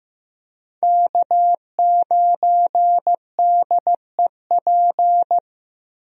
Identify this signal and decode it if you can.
Morse code, heard as a keyed tone, about 15 wpm, 700 Hz. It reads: K9DEP
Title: USA Call Signs - 5 Characters - Repeated 3x - 15wpm